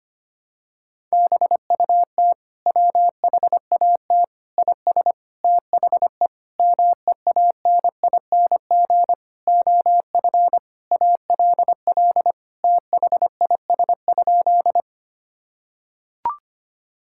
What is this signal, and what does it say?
Morse code, heard as a keyed tone, about 25 wpm, 700 Hz. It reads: BUT WHAT IS THE MEANING OF ALL THIS?
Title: Sentences from Top 1000 Words - Review Part 6 - 25wpm